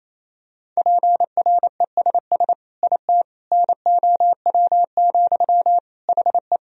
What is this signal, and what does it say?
Morse code, heard as a keyed tone, about 28 wpm, 700 Hz. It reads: PRESS IT NOW, HE
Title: Sentences from Top 1000 Words - Review Part 1 - 28wpm